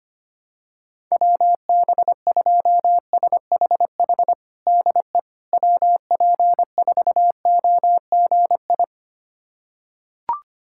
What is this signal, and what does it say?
Morse code, heard as a keyed tone, about 25 wpm, 700 Hz. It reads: WB2SHH DE WP4OGI
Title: USA Callsign DE Callsign - 6 Characters - 25wpm